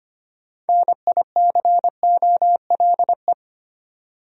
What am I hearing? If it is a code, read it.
Morse code, heard as a keyed tone, about 25 wpm, 700 Hz. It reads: NICOLE
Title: USA Common Names - 25wpm